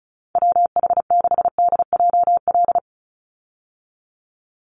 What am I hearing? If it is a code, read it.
Morse code, heard as a keyed tone, about 34 wpm, 700 Hz. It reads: WH6DJL
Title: USA Call Signs 35wpm